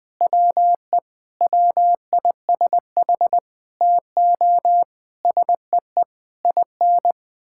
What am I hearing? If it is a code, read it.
Morse code, heard as a keyed tone, about 20 wpm, 700 Hz. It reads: WE WISH TO SEE IN
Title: Sets of 8 Words - 20wpm